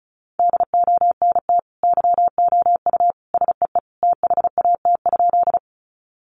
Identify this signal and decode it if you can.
Morse code, heard as a keyed tone, about 34 wpm, 700 Hz. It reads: DONT YOU SEE THAT?
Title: Sentences from Top 300 Words 35wpm